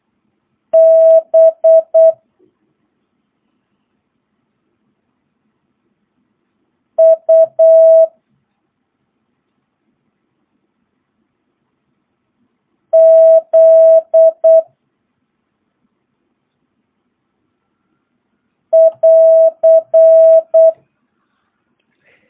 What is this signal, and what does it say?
Morse code, heard as a keyed tone, about 8 wpm, 650 Hz. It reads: B U Z <AR>
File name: jh8cbh / 2022cw